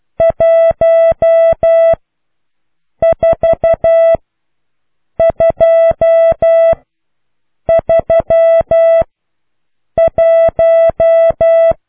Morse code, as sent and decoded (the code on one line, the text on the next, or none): .---- ....- ..--- ...-- .----
14231